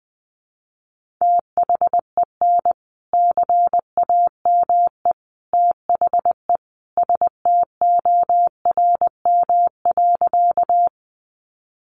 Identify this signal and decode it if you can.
Morse code, heard as a keyed tone, about 20 wpm, 700 Hz. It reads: THEN CAME THE STORM.
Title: Sentences from Top 1500 Words 20wpm